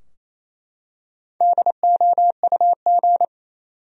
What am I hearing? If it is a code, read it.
Morse code, heard as a keyed tone, about 28 wpm, 700 Hz. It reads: DOUG
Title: USA Common Names - Max 4 Letters - 28wpm